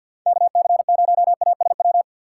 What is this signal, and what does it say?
Morse code, heard as a keyed tone, about 48 wpm, 700 Hz. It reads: KC0ASW
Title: USA Call Signs - 6 Characters - 4x Character Spacing - 50wpm